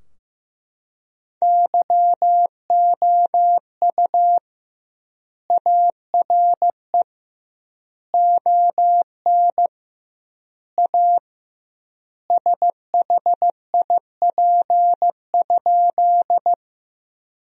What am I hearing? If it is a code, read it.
Morse code, heard as a keyed tone, about 15 wpm, 700 Hz. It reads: YOU ARE ON A SHIP?
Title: 2x Word-Spacing - Sentences from Top 400 Words - 15wpm